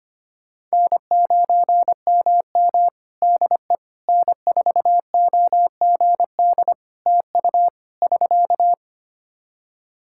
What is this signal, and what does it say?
Morse code, heard as a keyed tone, about 25 wpm, 700 Hz. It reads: N9MM DE N4OGD TU <SK>
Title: K1USN SST QSO Practice 25wpm